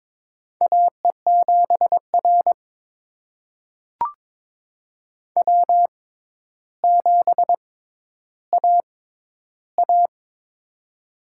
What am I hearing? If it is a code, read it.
Morse code, heard as a keyed tone, about 22 wpm, spaced as for 11 wpm, 700 Hz. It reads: AE7R W7AA
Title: USA Call Signs - 6x Character Spacing - 22wpm